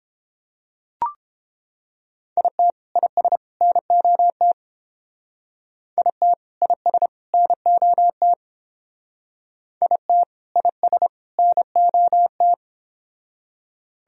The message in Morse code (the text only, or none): IT IS NOT IT IS NOT IT IS NOT